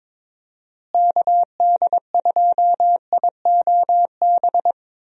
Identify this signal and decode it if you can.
Morse code, heard as a keyed tone, about 22 wpm, 700 Hz. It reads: KD2IOB
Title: USA Call Signs - 6 Characters - Repeated 3x - 22wpm